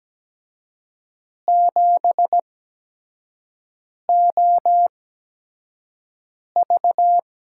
Morse code, heard as a keyed tone, about 17 wpm, spaced as for 5 wpm, 700 Hz. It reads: 7OV
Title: USA Call Signs - 6 Characters - 8x Character Spacing - 17wpm